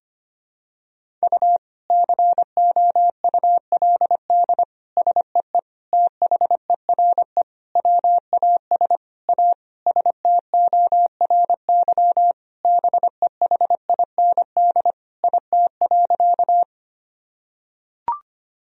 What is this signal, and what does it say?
Morse code, heard as a keyed tone, about 25 wpm, 700 Hz. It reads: U COULD SEE THERE WAS A STORY BEHIND IT.
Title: Sentences from Top 500 Words and Abbreviations 25wpm